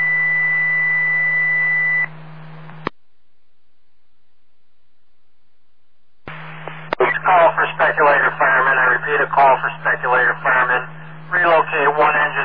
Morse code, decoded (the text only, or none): none